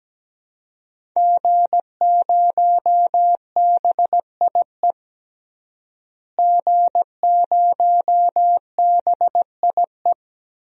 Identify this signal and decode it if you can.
Morse code, heard as a keyed tone, about 17 wpm, 700 Hz. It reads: G0BIE G0BIE
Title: DXCC-100 Call Signs - 5 Characters - Repeated 3x - 17wpm